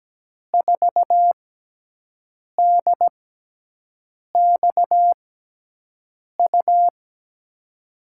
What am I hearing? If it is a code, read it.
Morse code, heard as a keyed tone, about 17 wpm, spaced as for 6 wpm, 700 Hz. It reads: 4DXU
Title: USA Call Signs - 5 Characters - 6x Character Spacing - 17wpm